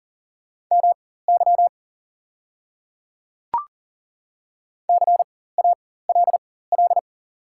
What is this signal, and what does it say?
Morse code, heard as a keyed tone, about 40 wpm, spaced as for 19 wpm, 700 Hz. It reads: MY CALL
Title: 4x Character Spacing - Top 100 Words - 40wpm